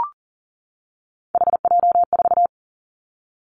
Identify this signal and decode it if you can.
Morse code, heard as a keyed tone, about 38 wpm, 700 Hz. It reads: HJ4
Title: Callsign Prefixes for Top 100 DXCC - 40wpm